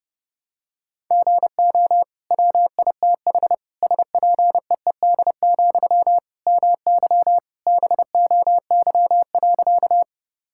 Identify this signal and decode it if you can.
Morse code, heard as a keyed tone, about 30 wpm, 700 Hz. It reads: GO WITH SPEED, MY BOY.
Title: Sentences from Top 1000 Words - Review Part 5 - 30wpm